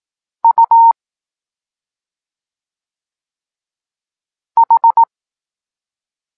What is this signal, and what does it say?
Morse code, heard as a keyed tone, about 18 wpm, 920 Hz. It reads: U H
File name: Tekst-9-rownomierny